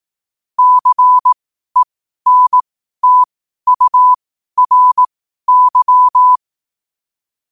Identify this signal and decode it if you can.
Morse code, heard as a keyed tone, about 18 wpm, spaced as for 12 wpm, 1000 Hz. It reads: CENTURY
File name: qotdfiles / 12wpm